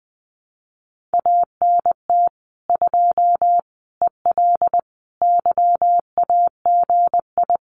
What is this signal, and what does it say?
Morse code, heard as a keyed tone, about 21 wpm, 700 Hz. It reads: ANT 2 EL YAGI
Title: Common QSO Phrases Encore 20wpm